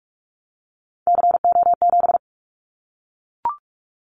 Morse code, heard as a keyed tone, about 44 wpm, 700 Hz. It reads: CO7
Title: Callsign Prefixes for Top 100 DXCC - 45wpm